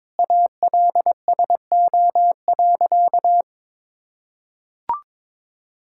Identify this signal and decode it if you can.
Morse code, heard as a keyed tone, about 22 wpm, 700 Hz. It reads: ALSO.
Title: Sentences from Top 1600 Words 22wpm